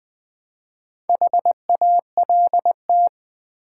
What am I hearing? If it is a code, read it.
Morse code, heard as a keyed tone, about 20 wpm, 700 Hz. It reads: HALT
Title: Build A Word - 1 Letter At A Time - 4-Letter Words - 20wpm